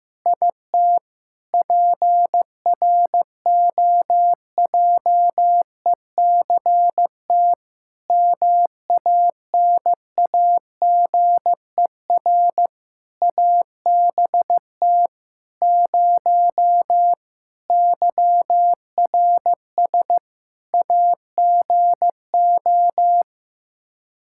Morse code, heard as a keyed tone, about 15 wpm, 700 Hz. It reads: IT PROJECT MANAGER ABT 0 YRS AGO
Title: Practice QSO - Exchange 3 - More Info - 15wpm